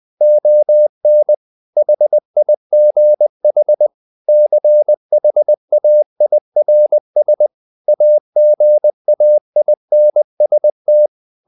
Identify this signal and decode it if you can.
Morse code, heard as a keyed tone, about 20 wpm, 600 Hz. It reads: ON HIGH CHAIRS AGAINST
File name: the old man and the sea part 2 20wpm